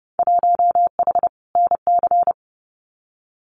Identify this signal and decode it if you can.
Morse code, heard as a keyed tone, about 30 wpm, 700 Hz. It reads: 1H NC
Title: Winter Field Day Exchanges 30wpm